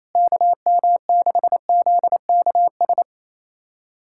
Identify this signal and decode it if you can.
Morse code, heard as a keyed tone, about 28 wpm, 700 Hz. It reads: KM6ZKS
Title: USA Callsign DE Callsign 28wpm